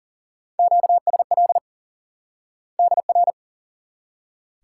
Morse code, heard as a keyed tone, about 40 wpm, 700 Hz. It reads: QSL DR
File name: IQE Course 3 - Lesson 89 - YAGI - Two Elements - 40wpm